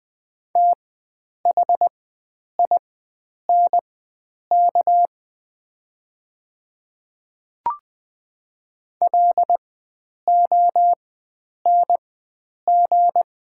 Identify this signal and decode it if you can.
Morse code, heard as a keyed tone, about 20 wpm, spaced as for 9 wpm, 700 Hz. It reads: THINK LONG
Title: 4x Character Spacing - Top 500 Words - 20wpm